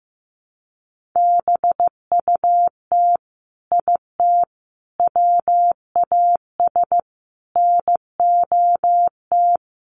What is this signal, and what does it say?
Morse code, heard as a keyed tone, about 15 wpm, 700 Hz. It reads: BUT IT WAS NOT
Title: Sentences from Top 1500 Words 15wpm